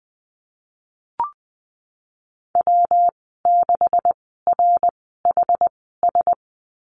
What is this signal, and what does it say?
Morse code, heard as a keyed tone, about 20 wpm, spaced as for 15 wpm, 700 Hz. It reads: W6RHS
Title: 179 Lesson 29g - Slash - Call Signs 20wpm